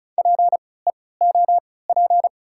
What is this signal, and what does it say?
Morse code, heard as a keyed tone, about 33 wpm, spaced as for 19 wpm, 690 Hz. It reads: PEOP
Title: 3x Character Spacing - Top 100 Words - 35wpm